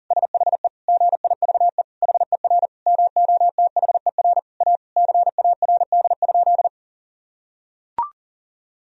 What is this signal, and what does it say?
Morse code, heard as a keyed tone, about 37 wpm, 700 Hz. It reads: SHE GIVE HER MOTHER A CARD?
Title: Sentences from Top 1000 Words - Review Part 4 - 40wpm